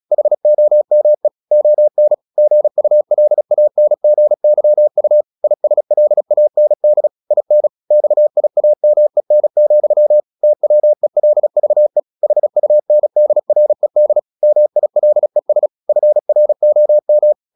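Morse code, heard as a keyed tone, about 36 wpm, 600 Hz. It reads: HOME ON GULANGYU ISLAND IN XIAMEN, TWELVE HUNDRED MILES FROM
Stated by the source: Morse practice at 36 wpm, text of Ken Follett's novel Never, from CW med Gnister.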